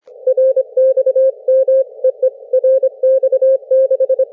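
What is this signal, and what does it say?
Morse code, heard as a keyed tone, about 25 wpm, 530 Hz. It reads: RXM EE RX6
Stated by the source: CW QSO between UA1CEG and RX6MY on homebrew rigs, three watts QRP into a delta antenna in forest tree branches, heard thousands of kilometres away; RU-QRP Club.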